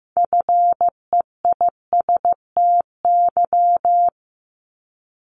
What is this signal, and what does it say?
Morse code, heard as a keyed tone, about 15 wpm, 690 Hz. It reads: FEISTY
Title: Multi-letter Phonemes 15wpm